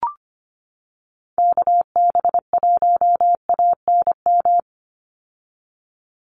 Morse code, heard as a keyed tone, about 25 wpm, 700 Hz. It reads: KB1ANM
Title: USA Call Signs 25wpm